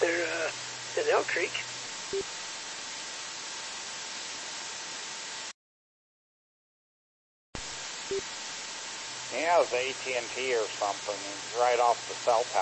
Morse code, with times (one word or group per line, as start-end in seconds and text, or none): none